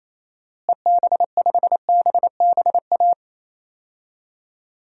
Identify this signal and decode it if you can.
Morse code, heard as a keyed tone, about 28 wpm, 700 Hz. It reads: EB5BBA
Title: DXCC-100 Call Signs - 6 Characters - Repeated 3x - 28wpm